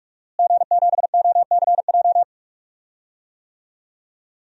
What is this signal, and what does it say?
Morse code, heard as a keyed tone, about 45 wpm, 700 Hz. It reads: G7OCJ
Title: DXCC-100 Call Signs - 5 Characters - Repeated 3x - 45wpm